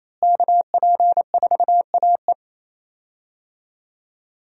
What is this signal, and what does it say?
Morse code, heard as a keyed tone, about 28 wpm, 700 Hz. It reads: KP4AE
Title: USA Call Signs - 5 Characters - Repeated 2x - 28wpm